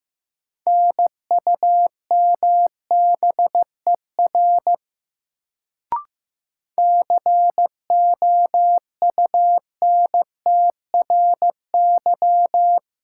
Morse code, heard as a keyed tone, about 15 wpm, 700 Hz. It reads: NUMBER COUNTRY
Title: Sets of 2 Words - New Word Drill - 15wpm